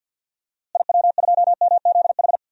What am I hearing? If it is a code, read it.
Morse code, heard as a keyed tone, about 50 wpm, 700 Hz. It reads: IW2MZH
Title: DXCC-100 Call Signs - 6 Characters - Repeated 3x - 50wpm